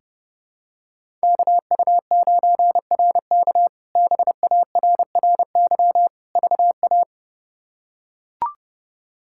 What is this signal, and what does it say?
Morse code, heard as a keyed tone, about 30 wpm, 700 Hz. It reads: KU9RK BARRY VA
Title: K1USN SST QSO Practice - 30wpm